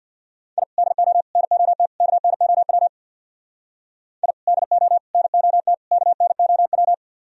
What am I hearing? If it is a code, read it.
Morse code, heard as a keyed tone, about 62 wpm, 700 Hz. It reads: I DO NOT KNOW I DO NOT KNOW
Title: Speed-Racing - Sets of 4 Words 50wpm